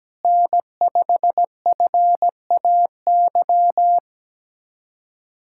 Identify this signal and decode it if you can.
Morse code, heard as a keyed tone, about 17 wpm, 700 Hz. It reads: N5FAY
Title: USA Callsign DE Callsign - 5 Characters - 17wpm